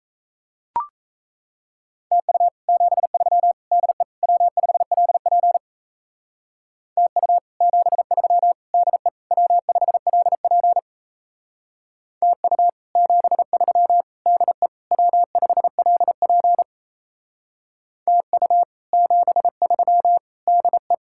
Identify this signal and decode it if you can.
Morse code, heard as a keyed tone, about 36 wpm, 700 Hz. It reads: TU 73 DE W5LP TU 73 DE W5LP TU 73 DE W5LP TU 73 DE